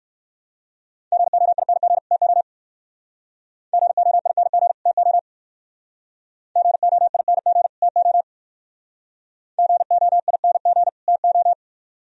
Morse code, heard as a keyed tone, about 58 wpm, 700 Hz. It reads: GOING TO GOING TO GOING TO GOING TO